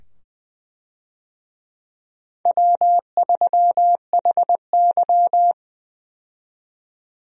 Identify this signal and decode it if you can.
Morse code, heard as a keyed tone, about 20 wpm, 700 Hz. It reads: W3HY
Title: USA Call Signs - 4 Characters - 20wpm